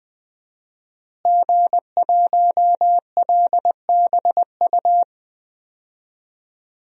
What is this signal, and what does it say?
Morse code, heard as a keyed tone, about 20 wpm, 700 Hz. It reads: G1LBU
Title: DXCC-100 Call Signs - 5 Characters - Repeated 2x - 20wpm